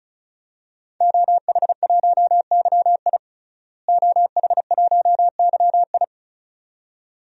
Morse code, heard as a keyed tone, about 36 wpm, 700 Hz. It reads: OH1YI OH1YI
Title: DXCC-100 Call Signs - 5 Characters - Repeated 2x - 35wpm